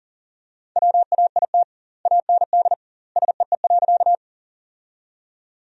Morse code, heard as a keyed tone, about 40 wpm, 700 Hz. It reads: WAIT AND SEE.
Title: 2x Word-Spacing - Sentences from Top 500 Words - 40wpm